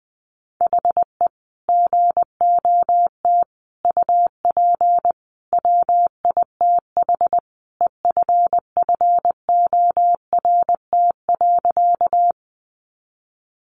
Morse code, heard as a keyed tone, about 20 wpm, 700 Hz. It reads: HE GOT UP WITH EFFORT.
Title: Sentences from Top 1200 Words 20wpm